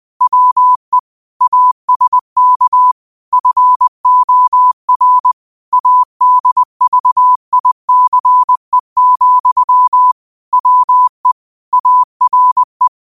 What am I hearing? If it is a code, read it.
Morse code, heard as a keyed tone, about 20 wpm, 1000 Hz. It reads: WE ASK FOR ADVICE, WE ARE